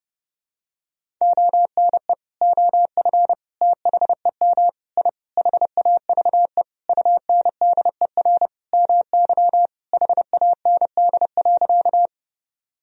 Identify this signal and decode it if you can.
Morse code, heard as a keyed tone, about 30 wpm, 700 Hz. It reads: ONE OF THEM I HAVE UNDER MY HAND.
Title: Sentences from Top 1000 Words - Review Part 1 - 30wpm